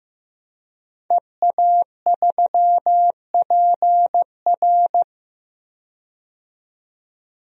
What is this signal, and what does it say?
Morse code, heard as a keyed tone, about 15 wpm, 700 Hz. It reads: EA3PR
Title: DXCC-100 Call Signs - 5 Characters - 15wpm